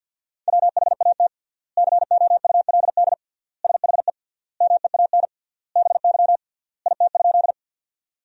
Morse code, heard as a keyed tone, about 50 wpm, 700 Hz. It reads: WHAT COULD SHE MEAN BY IT?
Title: 3x Word-Spacing - Sentences from Top 200 Words - 50wpm